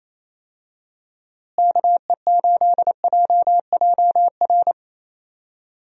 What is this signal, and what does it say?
Morse code, heard as a keyed tone, about 27 wpm, 700 Hz. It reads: KE8JJR
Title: USA Call Signs - 6 Characters - Repeated 3x - 28wpm